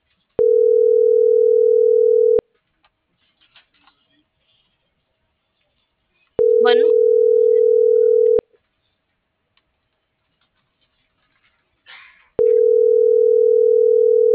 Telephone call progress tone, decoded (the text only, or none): ringback tone